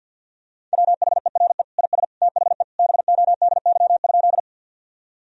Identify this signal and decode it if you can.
Morse code, heard as a keyed tone, about 50 wpm, 700 Hz. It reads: WHERE IS THE BODY?
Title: Sentences from Top 1000 Words - Review Part 2 - 50wpm